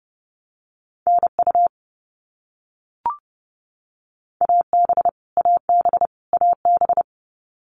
Morse code, heard as a keyed tone, about 30 wpm, 700 Hz. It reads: NU AB AB AB
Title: Canadian Provinces and Territories - Repeated 3x - 30wpm